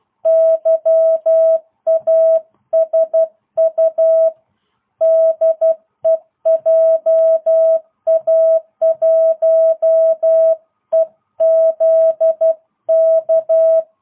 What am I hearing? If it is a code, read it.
Morse code, heard as a keyed tone, about 12 wpm, 650 Hz. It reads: YASU DEJA1EZK